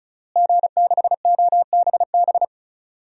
Morse code, heard as a keyed tone, about 36 wpm, 700 Hz. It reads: G6OBB
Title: DXCC-100 Call Signs - 5 Characters - Repeated 2x - 35wpm